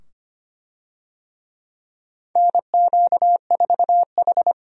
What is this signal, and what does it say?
Morse code, heard as a keyed tone, about 25 wpm, 700 Hz. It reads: NQ4H
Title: USA Call Signs - 4 Characters - 25wpm